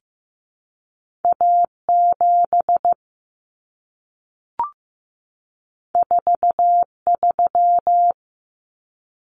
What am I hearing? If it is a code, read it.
Morse code, heard as a keyed tone, about 15 wpm, 700 Hz. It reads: A7 43
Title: Common Cut Numbers - 2-Digit 15wpm